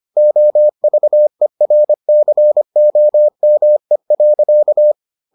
Morse code, heard as a keyed tone, about 25 wpm, 600 Hz. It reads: OVERCOME.